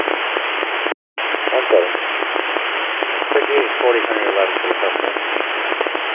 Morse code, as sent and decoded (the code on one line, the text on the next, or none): none